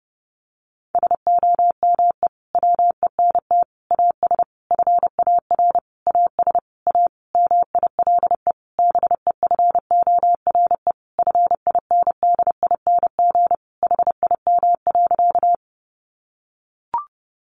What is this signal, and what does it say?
Morse code, heard as a keyed tone, about 30 wpm, 700 Hz. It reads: SOME WENT AS FAR AS A MILE BEFORE FINDING HIM.
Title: Sentences from Top 1100 Words 30wpm